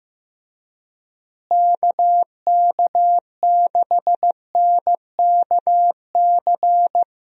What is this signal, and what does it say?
Morse code, heard as a keyed tone, about 15 wpm, 700 Hz. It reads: KK6NKC
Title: USA Call Signs - 6 Characters - Repeated 3x - 15wpm